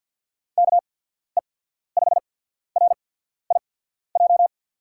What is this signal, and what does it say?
Morse code, heard as a keyed tone, about 47 wpm, spaced as for 14 wpm, 700 Hz. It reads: KE5RIJ
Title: USA Call Signs - 6 Characters - 8x Character Spacing - 50wpm